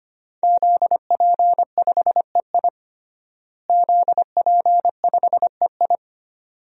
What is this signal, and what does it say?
Morse code, heard as a keyed tone, about 25 wpm, 700 Hz. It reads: ZP5EI ZP5EI
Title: DXCC-100 Call Signs - 5 Characters - Repeated 2x - 25wpm